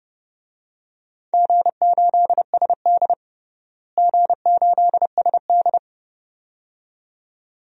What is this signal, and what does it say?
Morse code, heard as a keyed tone, about 30 wpm, 700 Hz. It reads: G8SD G8SD
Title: DXCC-100 Call Signs - 4 Characters - Repeated 2x - 30wpm